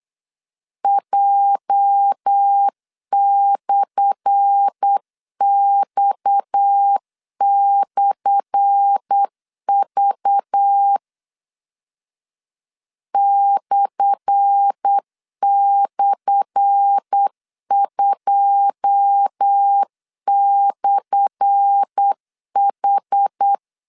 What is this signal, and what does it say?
Morse code, heard as a keyed tone, about 8 wpm, 800 Hz. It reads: J/X/V //2/H
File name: LEZIONI / 10 SIGNS / 10A SLASH 8wpm 3452